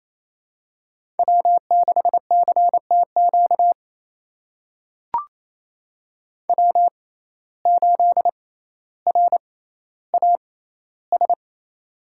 Morse code, heard as a keyed tone, about 27 wpm, spaced as for 14 wpm, 700 Hz. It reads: W6CTQ W8RAS
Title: USA Call Signs - 5 Characters - 6x Character Spacing - 28wpm